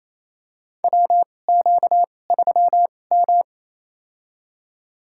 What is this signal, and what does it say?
Morse code, heard as a keyed tone, about 28 wpm, spaced as for 20 wpm, 700 Hz. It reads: WQ3M
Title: USA Call Signs - 4 Characters - 2x Character Spacing - 28wpm